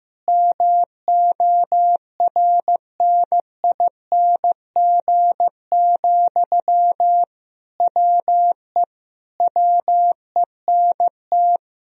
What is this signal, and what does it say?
Morse code, heard as a keyed tone, about 15 wpm, 700 Hz. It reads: MORNING, WE WENT
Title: Sentences from Top 1000 Words - Review Part 4 - 15wpm